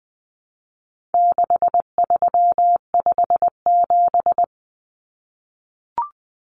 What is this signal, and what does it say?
Morse code, heard as a keyed tone, about 20 wpm, 700 Hz. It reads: 6357
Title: Common Cut Numbers - 4-Digit 20wpm